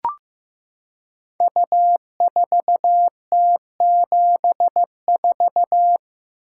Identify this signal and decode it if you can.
Morse code, heard as a keyed tone, about 15 wpm, 700 Hz. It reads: U4T74
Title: Code Groups - 5 Letter:Numbers - 15wpm